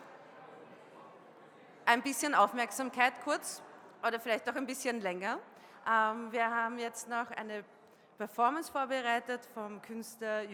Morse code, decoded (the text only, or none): none